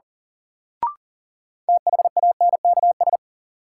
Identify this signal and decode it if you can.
Morse code, heard as a keyed tone, about 37 wpm, 700 Hz. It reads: THANKS
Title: Sets of 5 Words - New Word Drill - 40wpm